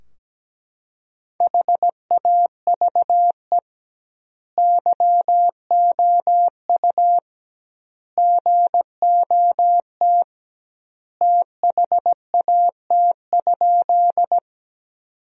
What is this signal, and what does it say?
Morse code, heard as a keyed tone, about 17 wpm, 700 Hz. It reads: HAVE YOU GOT THAT?
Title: 2x Word-Spacing - Sentences from Top 300 Words - 17wpm